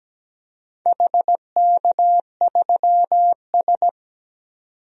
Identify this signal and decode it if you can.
Morse code, heard as a keyed tone, about 17 wpm, 700 Hz. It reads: HK3S
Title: DXCC-100 Call Signs - 4 Characters - 17wpm